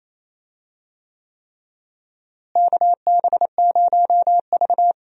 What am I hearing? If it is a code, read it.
Morse code, heard as a keyed tone, about 28 wpm, 700 Hz. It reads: KB0V
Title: USA Call Signs - 4 Characters - Repeated 3x - 28wpm